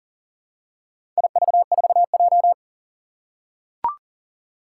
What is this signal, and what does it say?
Morse code, heard as a keyed tone, about 40 wpm, 700 Hz. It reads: IV4J
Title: Four Challenging Letter-Number Combinations 40wpm